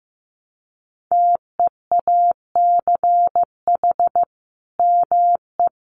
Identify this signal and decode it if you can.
Morse code, heard as a keyed tone, about 15 wpm, 700 Hz. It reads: TEACH ME
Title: Sentences from Top 500 Words 15wpm